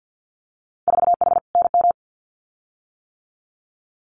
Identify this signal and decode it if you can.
Morse code, heard as a keyed tone, about 50 wpm, 700 Hz. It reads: 4H NM
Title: Winter Field Day Exchanges Encore 50wpm